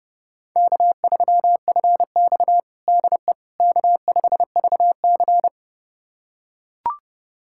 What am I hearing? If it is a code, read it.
Morse code, heard as a keyed tone, about 32 wpm, 700 Hz. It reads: K3FX DE K5VC
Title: USA Callsign DE Callsign - 4 Characters - 30wpm